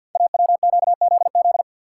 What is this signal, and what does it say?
Morse code, heard as a keyed tone, about 51 wpm, 700 Hz. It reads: AWQZZ
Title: Code Groups - 5 Letters - 50wpm